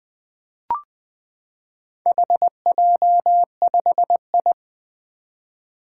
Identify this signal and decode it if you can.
Morse code, heard as a keyed tone, about 20 wpm, 700 Hz. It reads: HJ5I
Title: Four Challenging Letter-Number Combinations 20wpm